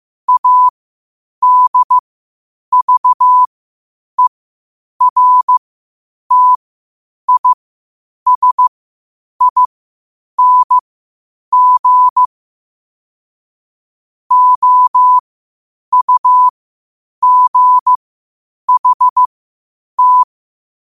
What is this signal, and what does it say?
Morse code, heard as a keyed tone, about 15 wpm, spaced as for 8 wpm, 1000 Hz. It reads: ADVERTISING OUGHT